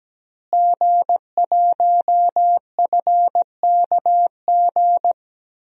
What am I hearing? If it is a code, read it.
Morse code, heard as a keyed tone, about 17 wpm, 700 Hz. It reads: G1FKG
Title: DXCC-100 Call Signs - 5 Characters - Repeated 2x - 17wpm